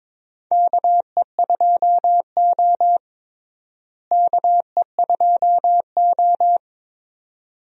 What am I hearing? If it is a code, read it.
Morse code, heard as a keyed tone, about 22 wpm, 700 Hz. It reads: KE2O KE2O
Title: USA Call Signs - 4 Characters - Repeated 2x - 22wpm